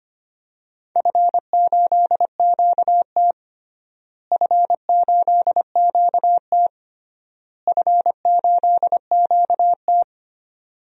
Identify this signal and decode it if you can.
Morse code, heard as a keyed tone, about 25 wpm, 700 Hz. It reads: F8QT F8QT F8QT
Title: DXCC-100 Call Signs - 4 Characters - Repeated 3x - 25wpm